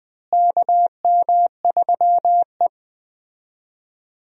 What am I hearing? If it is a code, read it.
Morse code, heard as a keyed tone, about 20 wpm, 700 Hz. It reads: KM3E
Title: USA Call Signs - 4 Characters - Repeated 3x - 20wpm